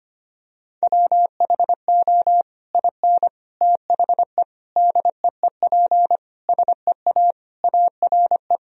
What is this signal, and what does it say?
Morse code, heard as a keyed tone, about 25 wpm, 700 Hz. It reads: WHO IN THE DEEP SEA ARE
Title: Sentences from Top 1000 Words - Review Part 3 - 25wpm